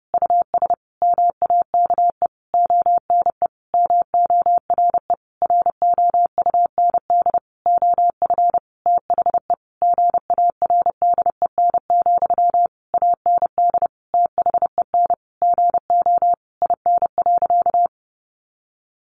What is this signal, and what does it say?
Morse code, heard as a keyed tone, about 28 wpm, 700 Hz. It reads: US MAKE ONE MORE ROUND OF THE GARDEN, AND THEN GO IN.
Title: Sentences from Top 700 Words 30wpm